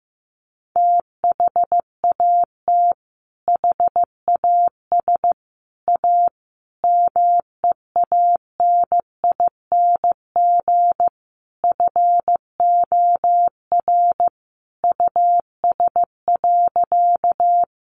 Morse code, heard as a keyed tone, about 15 wpm, 700 Hz. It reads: THAT HAS A MEANING FOR US.
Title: Sentences from Top 900 Words 15wpm